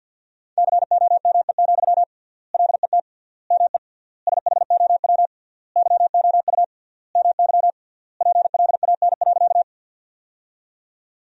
Morse code, heard as a keyed tone, about 53 wpm, 700 Hz. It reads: COME, LET ME SHOW YOU MY PLAN.
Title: 3x Word-Spacing - Sentences from Top 500 Words - 50wpm